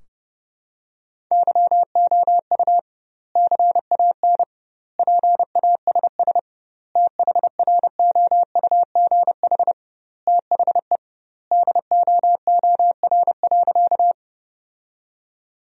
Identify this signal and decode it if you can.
Morse code, heard as a keyed tone, about 30 wpm, 700 Hz. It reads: YOU CAN PASS THROUGH THE DOOR.
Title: 2x Word-Spacing - Sentences from Top 400 Words - 30wpm